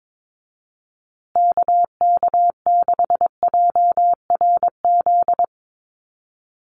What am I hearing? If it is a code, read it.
Morse code, heard as a keyed tone, about 22 wpm, 700 Hz. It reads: KK6JRZ
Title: USA Call Signs Repeated 2x 22wpm